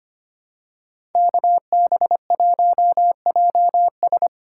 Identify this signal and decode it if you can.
Morse code, heard as a keyed tone, about 25 wpm, 700 Hz. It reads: KB1JS
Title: USA Call Signs - 2x Character Spacing - 25wpm